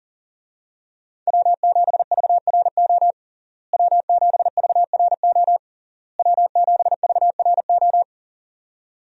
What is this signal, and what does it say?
Morse code, heard as a keyed tone, about 40 wpm, 700 Hz. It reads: W7VRO W7VRO W7VRO
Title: USA Call Signs - 5 Characters - Repeated 3x - 40wpm